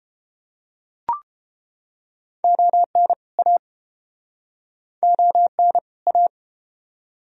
Morse code, heard as a keyed tone, about 31 wpm, 700 Hz. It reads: ON A ON A